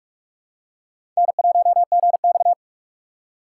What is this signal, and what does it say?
Morse code, heard as a keyed tone, about 46 wpm, 700 Hz. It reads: N1GX
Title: USA Call Signs - 4 Characters - 2x Character Spacing - 45wpm